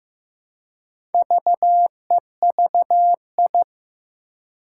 Morse code, heard as a keyed tone, about 15 wpm, 700 Hz. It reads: VEVI
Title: Four Challenging Letter-Number Combinations 15wpm